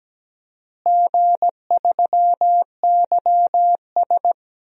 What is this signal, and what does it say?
Morse code, heard as a keyed tone, about 17 wpm, 700 Hz. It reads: G3YS
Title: DXCC-100 Call Signs - 4 Characters - Repeated 2x - 17wpm